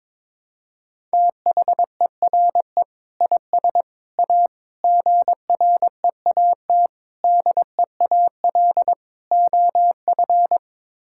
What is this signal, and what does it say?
Morse code, heard as a keyed tone, about 22 wpm, 700 Hz. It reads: THERE IS A GREAT DEAL OF
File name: practice / 22wpm / Sets of 6 Words - 22wpm